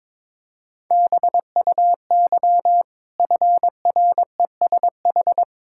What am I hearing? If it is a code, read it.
Morse code, heard as a keyed tone, about 22 wpm, 690 Hz. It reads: BUY FRESH
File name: practice / 22wpm / Easy Sentences with Hints - 22wpm